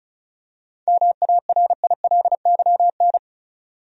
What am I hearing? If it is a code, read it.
Morse code, heard as a keyed tone, about 34 wpm, 700 Hz. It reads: MARILYN
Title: USA Common Names - 35wpm